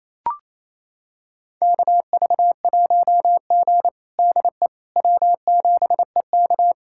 Morse code, heard as a keyed tone, about 28 wpm, 700 Hz. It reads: KV1G DE W7EK
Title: USA Callsign DE Callsign - 4 Characters - 28wpm